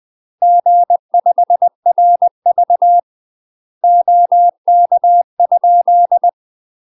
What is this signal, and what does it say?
Morse code, heard as a keyed tone, about 20 wpm, 700 Hz. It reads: G5RV OK?
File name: UK MASTER QSOs TRAINING / 20WPM / UK MASTER QSOs 0059